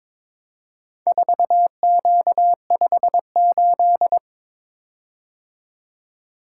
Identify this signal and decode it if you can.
Morse code, heard as a keyed tone, about 22 wpm, 700 Hz. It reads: 4Q58
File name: practice / 22wpm / Four Letter-Number Combinations - 22wpm